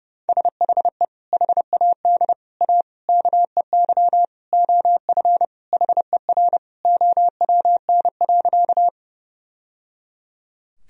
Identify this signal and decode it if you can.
Morse code, heard as a keyed tone, about 30 wpm, 700 Hz. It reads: SHE HAD A KEY OF HER OWN.
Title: Sentences from Top 1000 Words - Review Part 4 - 30wpm